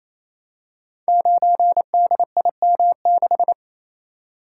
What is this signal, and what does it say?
Morse code, heard as a keyed tone, about 27 wpm, 700 Hz. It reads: 9DIM6
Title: Code Groups - 5 Letter:Numbers - 28wpm